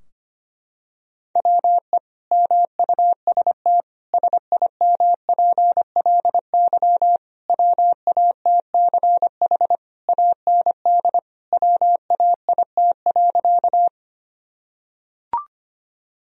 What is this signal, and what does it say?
Morse code, heard as a keyed tone, about 25 wpm, 700 Hz. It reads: WE MUST SIMPLY WATCH AND WAIT.
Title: Sentences from Top 1000 Words - Review Part 4 - 25wpm